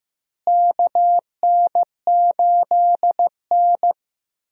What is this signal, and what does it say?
Morse code, heard as a keyed tone, about 15 wpm, 700 Hz. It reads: KN8N
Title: USA Call Signs - 4 Characters - Repeated 2x - 15wpm